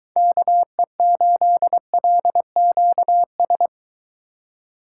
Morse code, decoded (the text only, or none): KE8LQS